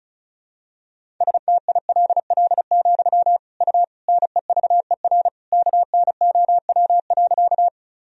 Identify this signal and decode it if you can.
Morse code, heard as a keyed tone, about 36 wpm, 700 Hz. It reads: STILL, U NEVER KNOW.
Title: Sentences from Top 300 Words and Abbreviations 35wpm